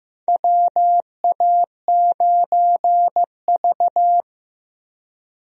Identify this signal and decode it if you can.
Morse code, heard as a keyed tone, about 15 wpm, 700 Hz. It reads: WA9V
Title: USA Call Signs - 4 Characters - Repeated 2x - 15wpm